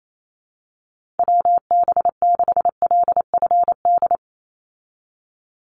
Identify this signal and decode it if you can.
Morse code, heard as a keyed tone, about 28 wpm, 700 Hz. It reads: WB6LFD
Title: USA Call Signs 28wpm